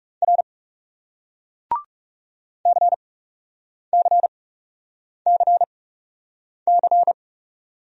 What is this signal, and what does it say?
Morse code, heard as a keyed tone, about 37 wpm, spaced as for 8 wpm, 700 Hz. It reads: R CCCC
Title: Speed-Racing Single Letters 30wpm